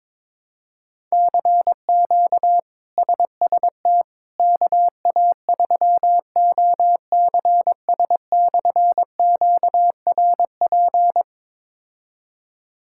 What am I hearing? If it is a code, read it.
Morse code, heard as a keyed tone, about 22 wpm, 700 Hz. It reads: CQ SST KA3OCS/QRP
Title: K1USN SST QSO Practice - 22wpm